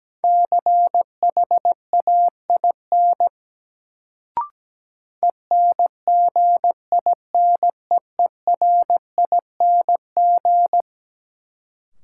Morse code, heard as a keyed tone, about 17 wpm, 700 Hz. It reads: CHAIN ENGINEERING
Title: Sentences from Top 1500 Words - New Word Drill - 17wpm